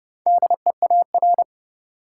Morse code, heard as a keyed tone, about 30 wpm, 700 Hz. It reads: DEAR
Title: Four Letter Words - 30wpm